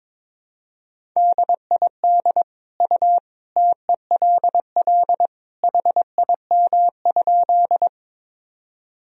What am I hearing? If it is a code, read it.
Morse code, heard as a keyed tone, about 22 wpm, 700 Hz. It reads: DID U TELL HIM?
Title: Sentences from Top 200 Words and Abbreviations 22wpm